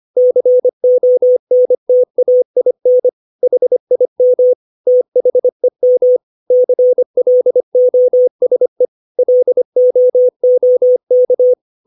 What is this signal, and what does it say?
Morse code, heard as a keyed tone, about 25 wpm, 500 Hz. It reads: CONTAIN HIM THEM CLOSE LOOK
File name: 632 words / 632 Words 25WPM